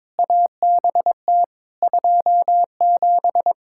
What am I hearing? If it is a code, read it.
Morse code, heard as a keyed tone, about 22 wpm, 700 Hz. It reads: ABT 27
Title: K1USN SST QSO Practice 22wpm